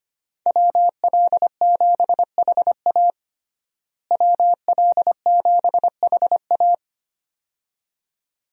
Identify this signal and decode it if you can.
Morse code, heard as a keyed tone, about 25 wpm, 700 Hz. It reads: WL7HA WL7HA
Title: USA Call Signs - 5 Characters - Repeated 2x - 25wpm